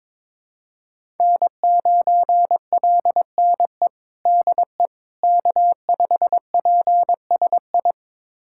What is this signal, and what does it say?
Morse code, heard as a keyed tone, about 22 wpm, 700 Hz. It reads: N9LNE DE K5PSI
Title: USA Callsign DE Callsign - 5 Characters - 22wpm